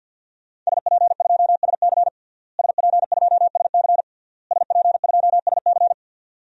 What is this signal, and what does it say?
Morse code, heard as a keyed tone, about 52 wpm, 700 Hz. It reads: SP2SC SP2SC SP2SC
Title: DXCC-100 Call Signs - 5 Characters - Repeated 3x - 50wpm